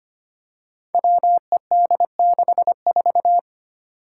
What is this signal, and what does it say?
Morse code, heard as a keyed tone, about 25 wpm, 700 Hz. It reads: WED64
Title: Five Challenging Letter-Number Combinations 25wpm